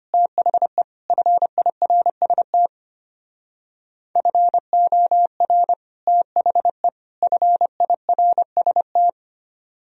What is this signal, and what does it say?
Morse code, heard as a keyed tone, about 32 wpm, 700 Hz. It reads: THE FIRST FOR THE FIRST